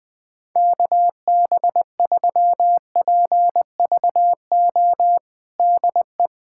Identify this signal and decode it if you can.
Morse code, heard as a keyed tone, about 21 wpm, 700 Hz. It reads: KB3PVO DE
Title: USA Callsign DE Callsign - 6 Characters - 20wpm